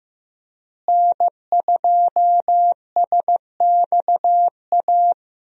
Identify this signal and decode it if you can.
Morse code, heard as a keyed tone, about 15 wpm, 700 Hz. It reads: N2SXA
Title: USA Call Signs - 5 Characters - Repeated 2x - 15wpm